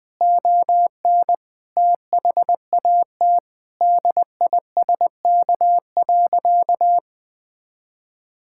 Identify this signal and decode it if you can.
Morse code, heard as a keyed tone, about 20 wpm, 700 Hz. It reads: ON THAT DISK.
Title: Sentences from Top 1000 Words - Review Part 6 - 20wpm